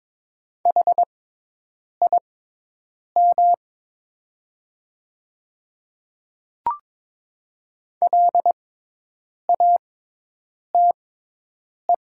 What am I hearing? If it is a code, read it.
Morse code, heard as a keyed tone, about 22 wpm, spaced as for 8 wpm, 700 Hz. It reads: HIM LATE
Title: 6x Character Spacing - Top 300 Words - 22wpm